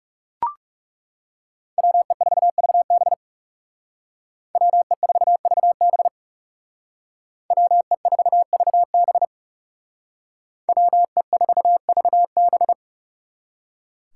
Morse code, heard as a keyed tone, about 36 wpm, 700 Hz. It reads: WE4VB WE4VB WE4VB WE4VB
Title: Speed Racing - USA 2x2 Call Signs 30wpm